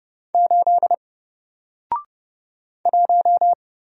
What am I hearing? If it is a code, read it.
Morse code, heard as a keyed tone, about 29 wpm, 700 Hz. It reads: 8 1
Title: Single Letter-Number Alt 30wpm